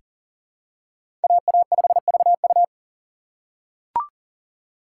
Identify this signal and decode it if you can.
Morse code, heard as a keyed tone, about 40 wpm, 700 Hz. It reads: AA5VU
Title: K1USN SST QSO Practice - 40wpm